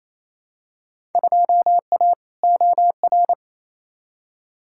Morse code, heard as a keyed tone, about 28 wpm, 700 Hz. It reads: 2A OR
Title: Field Day Exchanges Encore - 28wpm